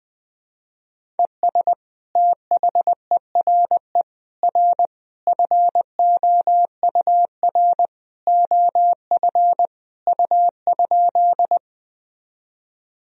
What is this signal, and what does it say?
Morse code, heard as a keyed tone, about 20 wpm, 700 Hz. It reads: ES THERE R FOUR OF U?
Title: Sentences from Top 300 Words and Abbreviations 20wpm